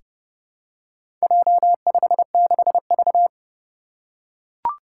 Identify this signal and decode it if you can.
Morse code, heard as a keyed tone, about 30 wpm, 700 Hz. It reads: J56V
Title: Four Challenging Letter-Number Combinations 30wpm